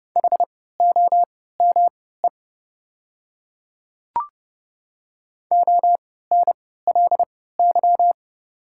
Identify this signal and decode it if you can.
Morse code, heard as a keyed tone, about 30 wpm, spaced as for 17 wpm, 700 Hz. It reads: HOME ONLY
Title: 3x Character Spacing - Top 300 Words - 30wpm